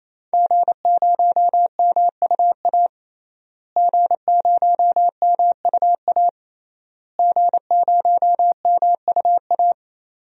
Morse code, heard as a keyed tone, about 28 wpm, 700 Hz. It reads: G0MUA G0MUA G0MUA
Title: DXCC-100 Call Signs - 5 Characters - Repeated 3x - 28wpm